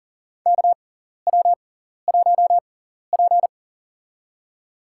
Morse code, heard as a keyed tone, about 40 wpm, spaced as for 14 wpm, 700 Hz. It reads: KW1P
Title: USA Call Signs - 6x Character Spacing - 40wpm